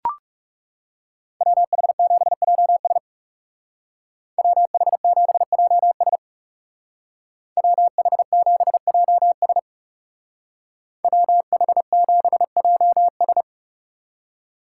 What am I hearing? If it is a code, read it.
Morse code, heard as a keyed tone, about 38 wpm, 700 Hz. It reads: WH7JS WH7JS WH7JS WH7JS
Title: Speed Racing - USA 2x2 Call Signs 30wpm